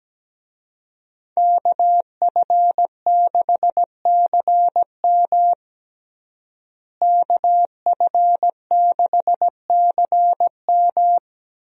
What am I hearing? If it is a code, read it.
Morse code, heard as a keyed tone, about 17 wpm, 700 Hz. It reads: KF6CM KF6CM
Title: USA Call Signs - 5 Characters - Repeated 2x - 17wpm